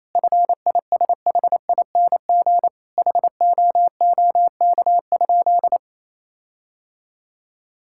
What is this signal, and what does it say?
Morse code, heard as a keyed tone, about 28 wpm, 700 Hz. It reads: FISHING HOOK?
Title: Sentences from Top 1600 Words 28wpm